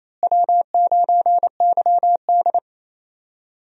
Morse code, heard as a keyed tone, about 28 wpm, 700 Hz. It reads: W9YD
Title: USA Call Signs - 4 Characters - Repeated 3x - 28wpm